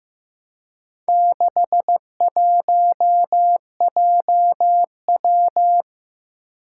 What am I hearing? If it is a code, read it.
Morse code, heard as a keyed tone, about 15 wpm, 700 Hz. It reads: 61JW
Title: Four Challenging Letter-Number Combinations 15wpm